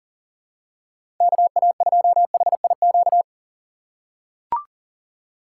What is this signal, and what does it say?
Morse code, heard as a keyed tone, about 37 wpm, 700 Hz. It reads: KA2HIQ
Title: USA Call Signs - 4x Character Spacing - 40wpm